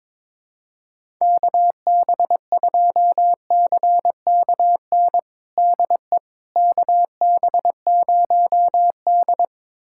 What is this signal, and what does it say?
Morse code, heard as a keyed tone, about 21 wpm, 700 Hz. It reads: KB2CKN DE KB0D
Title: USA Callsign DE Callsign 22wpm